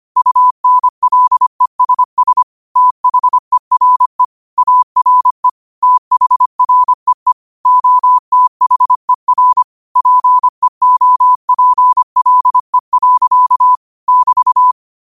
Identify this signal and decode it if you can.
Morse code, heard as a keyed tone, about 25 wpm, 1000 Hz. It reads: UNLESS THERE ARE THREE OTHER PEOPLE. <BT>
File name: qotdfiles / 25wpm